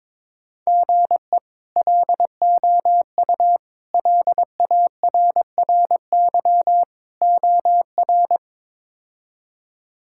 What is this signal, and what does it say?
Morse code, heard as a keyed tone, about 22 wpm, 700 Hz. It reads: GE LOU LARRY OR
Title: K1USN SST QSO Practice - 22wpm